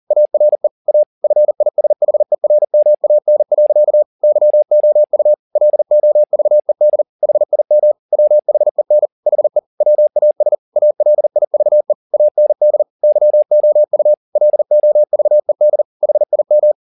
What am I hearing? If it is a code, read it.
Morse code, heard as a keyed tone, about 40 wpm, 600 Hz. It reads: ARE A FISHERMAN. YOU LOVED HIM WHEN HE WAS ALIVE AND YOU LOVED HIM
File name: the old man and the sea 2 40wpm